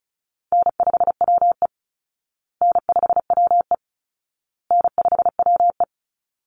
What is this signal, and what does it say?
Morse code, heard as a keyed tone, about 34 wpm, 700 Hz. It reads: N5WE N5WE N5WE
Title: Easy Call Signs Repeated 3x - 35wpm